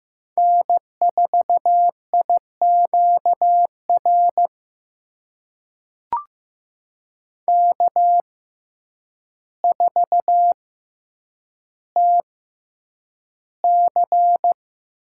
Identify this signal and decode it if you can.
Morse code, heard as a keyed tone, about 15 wpm, spaced as for 8 wpm, 700 Hz. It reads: N4IQR K4TC
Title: USA Call Signs - 6x Character Spacing - 15wpm